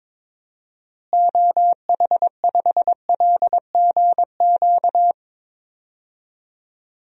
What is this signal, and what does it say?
Morse code, heard as a keyed tone, about 22 wpm, 700 Hz. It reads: OH5LGQ
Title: DXCC-100 Call Signs - 6 Characters - Repeated 2x - 22wpm